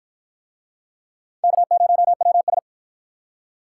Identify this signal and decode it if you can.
Morse code, heard as a keyed tone, about 46 wpm, 700 Hz. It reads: K0WS